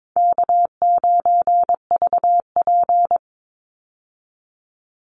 Morse code, heard as a keyed tone, about 22 wpm, 690 Hz. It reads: K9VP
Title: Easy Call Signs Repeated 2x - 22wpm